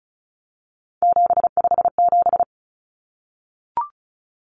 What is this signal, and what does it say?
Morse code, heard as a keyed tone, about 34 wpm, 700 Hz. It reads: 757
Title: Common Cut Numbers - 3-Digit 35wpm